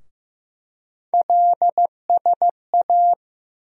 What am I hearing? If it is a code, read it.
Morse code, heard as a keyed tone, about 15 wpm, 700 Hz. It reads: LSA
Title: USA Call Sign Postfixes - 3 Letters 15wpm